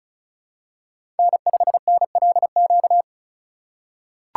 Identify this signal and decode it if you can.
Morse code, heard as a keyed tone, about 36 wpm, 700 Hz. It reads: N5NLQ
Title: USA Call Signs - 5 Characters - 3x Character Spacing - 35wpm